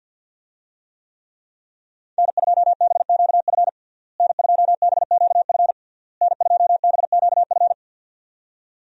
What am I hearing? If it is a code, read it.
Morse code, heard as a keyed tone, about 50 wpm, 700 Hz. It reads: N2BQF N2BQF N2BQF
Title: USA Call Signs - 5 Characters - Repeated 3x - 50wpm